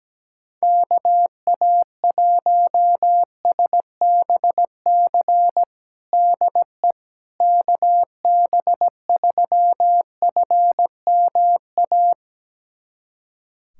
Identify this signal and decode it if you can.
Morse code, heard as a keyed tone, about 17 wpm, 700 Hz. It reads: KA1SBC DE KB3FMA
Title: USA Callsign DE Callsign - 6 Characters - 17wpm